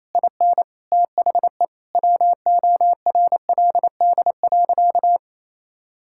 Morse code, heard as a keyed tone, about 28 wpm, 700 Hz. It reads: IN THE WORLD.
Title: Sentences from Top 1600 Words 28wpm